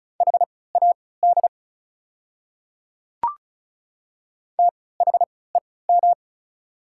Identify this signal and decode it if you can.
Morse code, heard as a keyed tone, about 36 wpm, spaced as for 20 wpm, 690 Hz. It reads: HAD THEM
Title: 3x Character Spacing - Top 100 Words - 35wpm